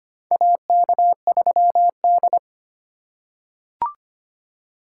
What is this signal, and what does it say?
Morse code, heard as a keyed tone, about 25 wpm, 700 Hz. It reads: AK3D
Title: USA Call Signs - 4 Characters - 3x Character Spacing - 25wpm